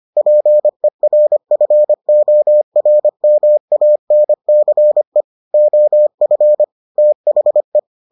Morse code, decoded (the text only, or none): PERFORMANCE OF THE